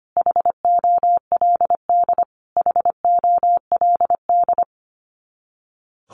Morse code, heard as a keyed tone, about 25 wpm, 700 Hz. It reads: HOLD HOLD
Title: Top 500 Words Repeated 25wpm